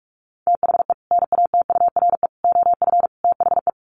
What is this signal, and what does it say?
Morse code, heard as a keyed tone, about 44 wpm, 700 Hz. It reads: THE NATURE OF THE
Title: Sets of 4 Words 45wpm